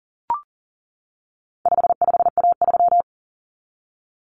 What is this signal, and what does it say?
Morse code, heard as a keyed tone, about 40 wpm, 700 Hz. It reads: 55A3
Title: Common Cut Numbers - 4-Digit 40wpm